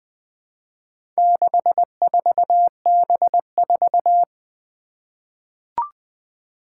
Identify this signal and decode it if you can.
Morse code, heard as a keyed tone, about 20 wpm, 700 Hz. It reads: 64B4
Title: Four Challenging Letter-Number Combinations 20wpm